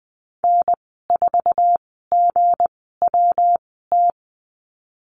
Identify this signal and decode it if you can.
Morse code, heard as a keyed tone, about 20 wpm, spaced as for 15 wpm, 700 Hz. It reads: N4GWT
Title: 186 Lesson 30g - Letter b - Call Signs 20wpm